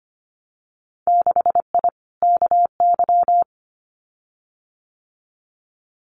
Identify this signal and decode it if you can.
Morse code, heard as a keyed tone, about 25 wpm, 700 Hz. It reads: 6I KY
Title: Winter Field Day Exchanges 25wpm